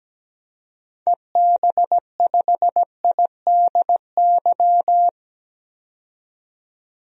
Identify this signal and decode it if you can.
Morse code, heard as a keyed tone, about 17 wpm, 700 Hz. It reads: EB5IDY
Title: DXCC-100 Call Signs - 6 Characters - 17wpm